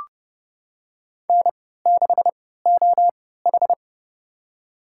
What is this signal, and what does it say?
Morse code, heard as a keyed tone, about 30 wpm, spaced as for 17 wpm, 700 Hz. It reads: N6OH
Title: USA Call Signs - 4 Characters - 3x Character Spacing - 30wpm